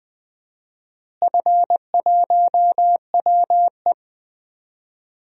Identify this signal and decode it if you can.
Morse code, heard as a keyed tone, about 20 wpm, 700 Hz. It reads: F1WE
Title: DXCC-100 Call Signs - 4 Characters - Repeated 2x - 20wpm